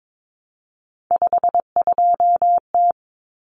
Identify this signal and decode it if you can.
Morse code, heard as a keyed tone, about 22 wpm, 700 Hz. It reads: 52T
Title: Common Cut Numbers - 3-Digit 22wpm